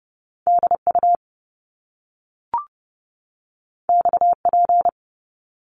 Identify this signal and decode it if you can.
Morse code, heard as a keyed tone, about 30 wpm, 700 Hz. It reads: DU XP
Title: Top Two Letter Combinations in Words 30wpm